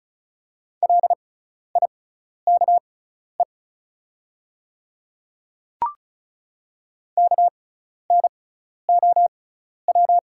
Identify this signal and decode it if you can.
Morse code, heard as a keyed tone, about 36 wpm, spaced as for 12 wpm, 690 Hz. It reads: LIKE KNOW
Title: 6x Character Spacing - Top 200 Words - 35wpm